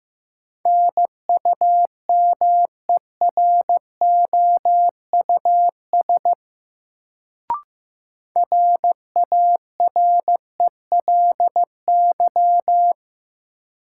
Morse code, heard as a keyed tone, about 15 wpm, 700 Hz. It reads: NUMEROUS RARELY
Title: Sentences from Top 1400 Words - New Word Drill - 15wpm